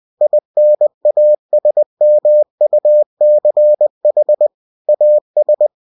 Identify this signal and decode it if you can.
Morse code, heard as a keyed tone, about 20 wpm, 600 Hz. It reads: INASMUCH AS
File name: cw-books / winnie-the-pooh / winnie-20wpm-page-0026